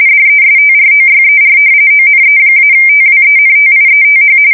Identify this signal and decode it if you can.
RTTY, about 45.45 baud. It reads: DESTROYER ONE-TWENTIETH
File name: fsk170 / battleshipnj